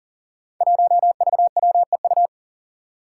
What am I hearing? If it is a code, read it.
Morse code, heard as a keyed tone, about 40 wpm, 700 Hz. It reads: 1VWEU
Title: Five Challenging Letter-Number Combinations 40wpm